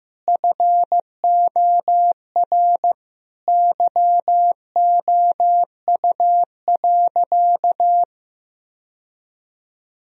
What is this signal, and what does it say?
Morse code, heard as a keyed tone, about 15 wpm, 700 Hz. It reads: FOR YOU.
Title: Sentences from Top 1000 Words - Review Part 3 - 15wpm